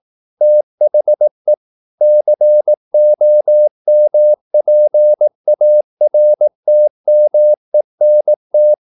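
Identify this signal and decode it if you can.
Morse code, heard as a keyed tone, about 18 wpm, 600 Hz. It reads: THE COMPARTMENT
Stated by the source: Morse practice at 18 wpm, English text, from CW med Gnister.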